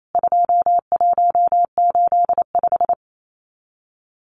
Morse code, heard as a keyed tone, about 28 wpm, 700 Hz. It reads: 2185
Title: CWT Exchange 28wpm